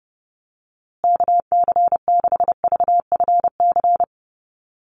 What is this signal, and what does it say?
Morse code, heard as a keyed tone, about 30 wpm, 700 Hz. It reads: KC6VFC
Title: USA Call Signs Repeated 2x 30wpm